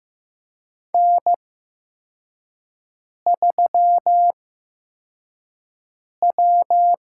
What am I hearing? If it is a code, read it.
Morse code, heard as a keyed tone, about 15 wpm, spaced as for 4 wpm, 700 Hz. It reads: N3W
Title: USA Call Signs - 8x Character Spacing - 15wpm